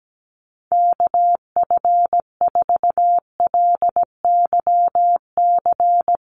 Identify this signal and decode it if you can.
Morse code, heard as a keyed tone, about 17 wpm, 700 Hz. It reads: KF4LYC
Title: USA Call Signs Repeated 2x 17wpm